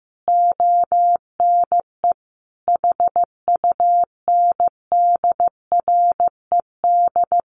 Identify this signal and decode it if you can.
Morse code, heard as a keyed tone, about 15 wpm, 700 Hz. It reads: ONE HUNDRED
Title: A Sound Of Thunder 15wpm